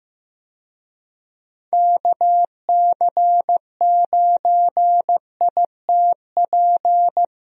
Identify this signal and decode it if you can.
Morse code, heard as a keyed tone, about 15 wpm, 700 Hz. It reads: KC9ITP
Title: USA Call Signs - 6 Characters - 15wpm